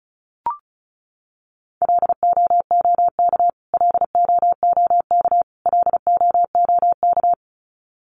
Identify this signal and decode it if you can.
Morse code, heard as a keyed tone, about 34 wpm, 700 Hz. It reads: LOOK LOOK LOOK
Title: Top 200 Words Repeated 35wpm